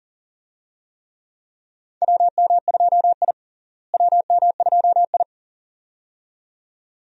Morse code, heard as a keyed tone, about 37 wpm, 700 Hz. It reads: WM2I WM2I
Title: USA Call Signs - 4 Characters - Repeated 2x - 40wpm